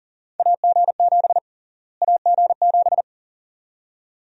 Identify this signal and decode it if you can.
Morse code, heard as a keyed tone, about 40 wpm, 700 Hz. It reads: AG7 AG7
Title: USA Call Sign Prefixes - Repeated 2x - 40wpm